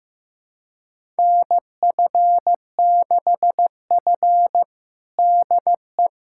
Morse code, heard as a keyed tone, about 15 wpm, 700 Hz. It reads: NF6F DE
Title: USA Callsign DE Callsign - 4 Characters - 15wpm